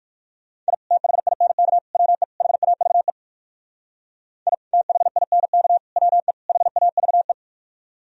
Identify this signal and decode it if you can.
Morse code, heard as a keyed tone, about 59 wpm, 700 Hz. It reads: I THINK WE HAVE I THINK WE HAVE